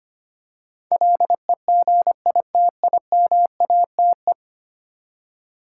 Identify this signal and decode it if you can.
Morse code, heard as a keyed tone, about 25 wpm, 700 Hz. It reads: LEGITIMATE
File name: Ten Letter Words - 25wpm